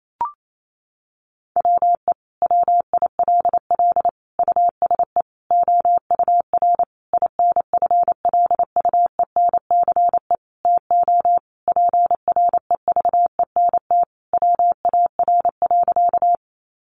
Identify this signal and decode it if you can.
Morse code, heard as a keyed tone, about 28 wpm, 700 Hz. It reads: WE WILL USE OUR INFLUENCE TO PREVENT WAR.
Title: Sentences from Top 1100 Words 28wpm